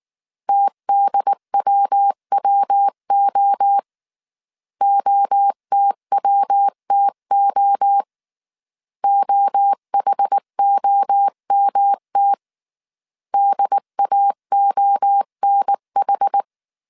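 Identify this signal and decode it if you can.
Morse code, heard as a keyed tone, about 19 wpm, 800 Hz. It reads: TDWWO OTWTO OHOMT DAONH
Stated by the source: Morse code dictation at 18 wpm from the IW3SGT QRS Morse code training course, five-character groups of 300 characters.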